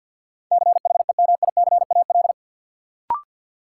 Morse code, heard as a keyed tone, about 47 wpm, 700 Hz. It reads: CHEMICAL
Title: Sentences from Top 1200 Words - New Word Drill - 50wpm